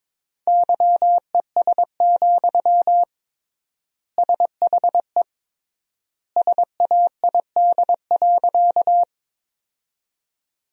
Morse code, heard as a keyed tone, about 22 wpm, 700 Hz. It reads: YES, SHE SAID.
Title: 3x Word-Spacing - Sentences from Top 500 Words - 22wpm